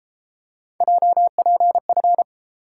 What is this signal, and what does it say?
Morse code, heard as a keyed tone, about 32 wpm, 700 Hz. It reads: JPF